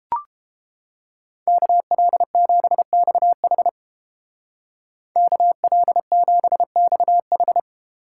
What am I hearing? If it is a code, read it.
Morse code, heard as a keyed tone, about 29 wpm, 700 Hz. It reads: KL7XH KL7XH